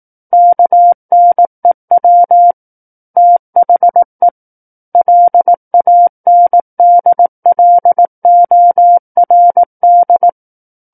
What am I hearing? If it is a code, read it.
Morse code, heard as a keyed tone, about 18 wpm, 700 Hz. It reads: KNEW THE LANDLORD
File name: 18 WPM